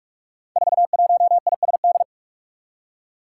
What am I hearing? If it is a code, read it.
Morse code, heard as a keyed tone, about 44 wpm, 700 Hz. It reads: 41ISD
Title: Five Challenging Letter-Number Combinations 45wpm